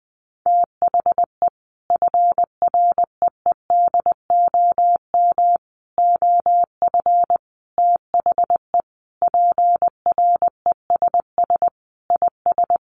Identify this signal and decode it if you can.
Morse code, heard as a keyed tone, about 19 wpm, 700 Hz. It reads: THE FREEDOM OF THE PRESS IS
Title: Sentences from Top 800 Words 20wpm